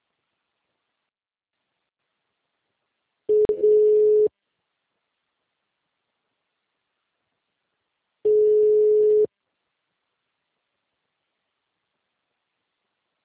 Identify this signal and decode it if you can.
Ringback tone (Europe, 425 Hz).